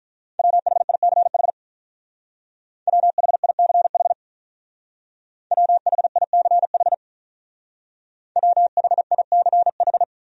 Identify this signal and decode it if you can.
Morse code, heard as a keyed tone, about 45 wpm, 700 Hz. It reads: WHICH WHICH WHICH WHICH